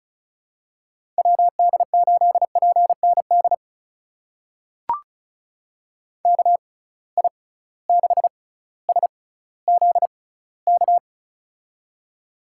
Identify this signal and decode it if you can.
Morse code, heard as a keyed tone, about 34 wpm, spaced as for 18 wpm, 700 Hz. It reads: WD8PND KI6SZK
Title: USA Call Signs - 6 Characters - 6x Character Spacing - 35wpm